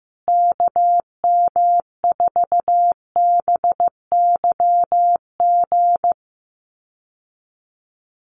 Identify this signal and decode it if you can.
Morse code, heard as a keyed tone, about 15 wpm, 700 Hz. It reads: KM4BYG
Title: USA Call Signs Repeated 3x 15wpm